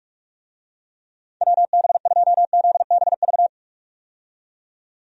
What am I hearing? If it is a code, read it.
Morse code, heard as a keyed tone, about 43 wpm, 700 Hz. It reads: WB2ZBV
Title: USA Call Signs - 6 Characters - 45wpm